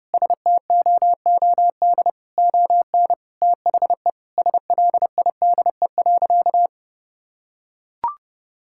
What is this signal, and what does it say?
Morse code, heard as a keyed tone, about 32 wpm, 700 Hz. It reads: STOOD ON THE SLIDE.
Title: Sentences from Top 1600 Words 30wpm